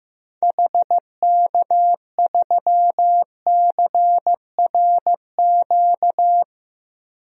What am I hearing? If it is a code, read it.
Morse code, heard as a keyed tone, about 15 wpm, 700 Hz. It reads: HK3CRQ
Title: DXCC-100 Call Signs - 6 Characters - 15wpm